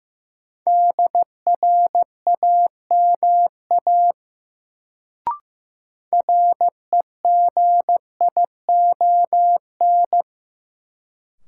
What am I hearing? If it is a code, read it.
Morse code, heard as a keyed tone, about 15 wpm, 700 Hz. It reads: DRAMA REGION
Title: Sentences from Top 1500 Words - New Word Drill Alt - 15wpm